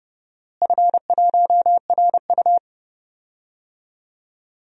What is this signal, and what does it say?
Morse code, heard as a keyed tone, about 31 wpm, 700 Hz. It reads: F1RU
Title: DXCC-100 Call Signs - 4 Characters - Repeated 3x - 30wpm